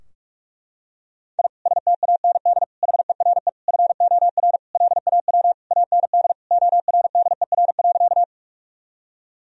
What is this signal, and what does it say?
Morse code, heard as a keyed tone, about 41 wpm, 700 Hz. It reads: I STAND HERE FOR LAW AND ORDER.
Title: Sentences from Top 1000 Words - Review Part 3 - 45wpm